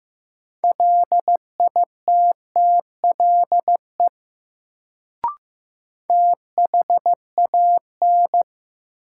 Morse code, heard as a keyed tone, about 15 wpm, 700 Hz. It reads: LITTLE THAN
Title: Sets of 2 Words - New Word Drill - 15wpm